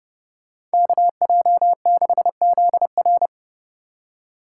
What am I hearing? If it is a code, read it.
Morse code, heard as a keyed tone, about 31 wpm, 700 Hz. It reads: KJ6ZR